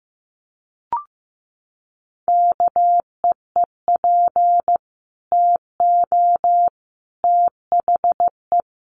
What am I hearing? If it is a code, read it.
Morse code, heard as a keyed tone, about 15 wpm, 700 Hz. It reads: KEEP TO THE
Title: Sentences from Top 300 Words 15wpm